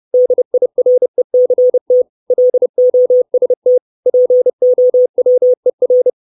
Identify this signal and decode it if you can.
Morse code, heard as a keyed tone, about 30 wpm, 500 Hz. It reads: DIRECT LOST POWER
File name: 632 Words 30WPM